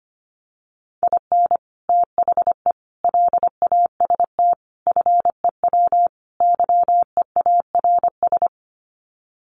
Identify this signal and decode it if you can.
Morse code, heard as a keyed tone, about 25 wpm, 700 Hz. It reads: IN THE LAST FEW YEARS
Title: Sets of 5 Words 25wpm